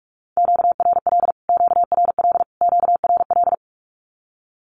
Morse code, heard as a keyed tone, about 44 wpm, 700 Hz. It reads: QRL QRL QRL
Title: Q-Codes Repeated 3x 45wpm